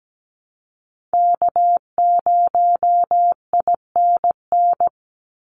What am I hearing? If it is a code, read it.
Morse code, heard as a keyed tone, about 17 wpm, 700 Hz. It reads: K0INN
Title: Call Signs Repeated 2x - 17wpm